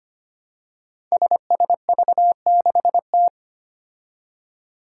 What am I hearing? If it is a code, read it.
Morse code, heard as a keyed tone, about 25 wpm, 700 Hz. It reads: SSV6T
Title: Five Challenging Letter-Number Combinations 25wpm